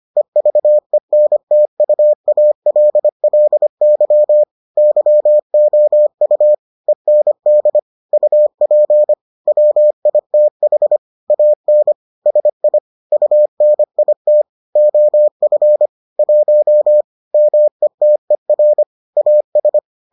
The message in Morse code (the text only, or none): EVENTUALLY YOU END UP WITH AN SI UNIT OF 1 METER AS